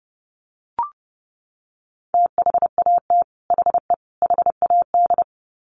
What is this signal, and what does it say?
Morse code, heard as a keyed tone, about 30 wpm, 700 Hz. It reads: THAT HE HAD
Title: Sets of 3 Words 30wpm